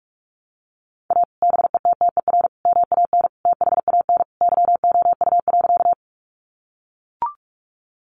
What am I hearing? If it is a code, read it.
Morse code, heard as a keyed tone, about 44 wpm, 700 Hz. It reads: A BETTER MAN THAN YOU.
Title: Sentences from Top 400 Words 45wpm